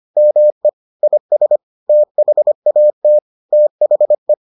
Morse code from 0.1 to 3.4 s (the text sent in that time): ME IS THAT